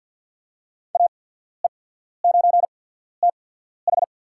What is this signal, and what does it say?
Morse code, heard as a keyed tone, about 52 wpm, spaced as for 14 wpm, 700 Hz. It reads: AE9TH
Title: USA Call Signs - 5 Characters - 8x Character Spacing - 50wpm